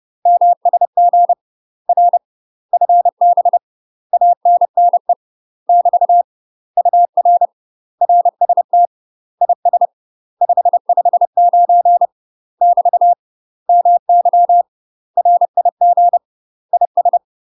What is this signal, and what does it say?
Morse code, heard as a keyed tone, about 30 wpm, 700 Hz. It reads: MSG R FB ANNE <BT> UR RST IS 559 <BT> MY RIG IS
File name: UK MASTER QSOs TRAINING / 30WPM / UK MASTER QSOs 0015